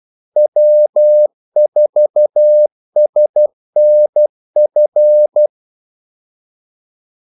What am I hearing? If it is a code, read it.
Morse code, heard as a keyed tone, about 12 wpm, 600 Hz. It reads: W4SNF
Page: 2 ekstra ordmellemrum mellem kaldesignalerne